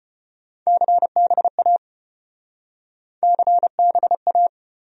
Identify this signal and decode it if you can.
Morse code, heard as a keyed tone, about 33 wpm, 700 Hz. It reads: CBA CBA